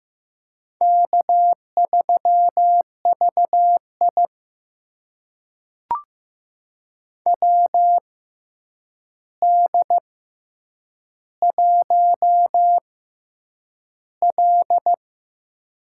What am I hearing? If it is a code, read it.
Morse code, heard as a keyed tone, about 15 wpm, spaced as for 8 wpm, 700 Hz. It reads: K3VI WD1L
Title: USA Call Signs - 4 Characters - 6x Character Spacing - 15wpm